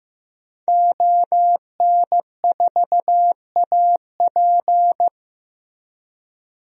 Morse code, heard as a keyed tone, about 16 wpm, 700 Hz. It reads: ON4AP
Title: DXCC-100 Call Signs - 5 Characters - Repeated 2x - 15wpm